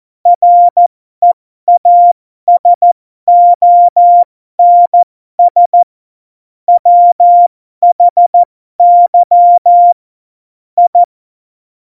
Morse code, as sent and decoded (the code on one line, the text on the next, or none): .-. .- ... --- -. ... / .-- .... -.-- / ..
REASONS WHY I